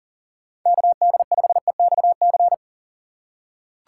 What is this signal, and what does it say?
Morse code, heard as a keyed tone, about 40 wpm, 700 Hz. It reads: KD5EXC
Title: USA Call Signs - 2x Character Spacing - 40wpm